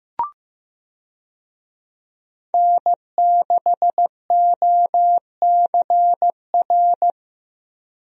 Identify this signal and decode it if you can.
Morse code, heard as a keyed tone, about 15 wpm, 700 Hz. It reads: N6OCR
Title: USA Call Signs - 5 Characters - Repeated 2x - 15wpm